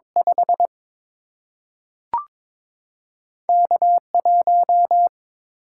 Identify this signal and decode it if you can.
Morse code, heard as a keyed tone, about 22 wpm, 700 Hz. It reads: 5 K1
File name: practice / 22wpm / Two Letter-Number Combinations - 22wpm